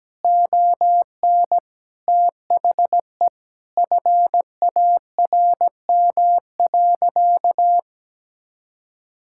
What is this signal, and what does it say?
Morse code, heard as a keyed tone, about 17 wpm, 700 Hz. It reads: ON THE FARM.
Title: Sentences from Top 1000 Words - Review Part 6 - 17wpm